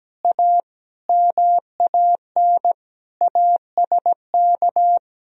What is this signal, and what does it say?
Morse code, heard as a keyed tone, about 17 wpm, 700 Hz. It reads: A MAN ASK
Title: Sentences from Top 200 Words and Abbreviations 17wpm